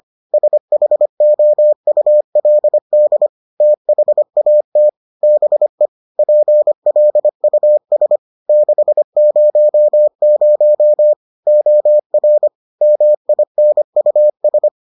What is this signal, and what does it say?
Morse code, heard as a keyed tone, about 25 wpm, 600 Hz. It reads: SHOULD THAT BE PLUS 600 OR MINUS